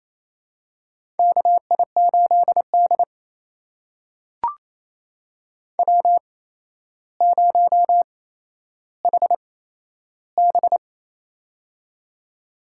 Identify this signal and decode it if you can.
Morse code, heard as a keyed tone, about 28 wpm, spaced as for 12 wpm, 700 Hz. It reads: KI8D W0HB
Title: USA Call Signs - 4 Characters - 8x Character Spacing - 28wpm